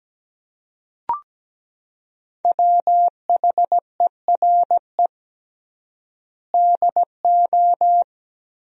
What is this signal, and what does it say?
Morse code, heard as a keyed tone, about 17 wpm, 700 Hz. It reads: WHERE DO
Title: 3x Word-Spacing - Sentences from Top 200 Words - 17wpm